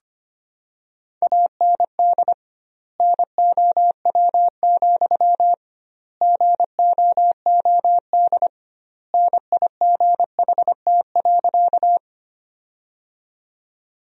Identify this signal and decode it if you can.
Morse code, heard as a keyed tone, about 25 wpm, 700 Hz. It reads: AND NOW, GOOD NIGHT.
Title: 2x Word-Spacing - Sentences from Top 300 Words - 25wpm